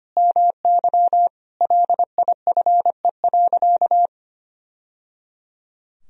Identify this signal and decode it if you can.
Morse code, heard as a keyed tone, about 25 wpm, 700 Hz. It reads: MY LIFE.
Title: Sentences from Top 1000 Words - Review Part 3 - 25wpm